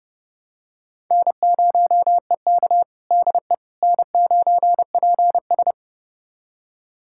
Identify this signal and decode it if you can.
Morse code, heard as a keyed tone, about 30 wpm, 700 Hz. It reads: N0EK DE N9PS
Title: USA Callsign DE Callsign - 4 Characters - 30wpm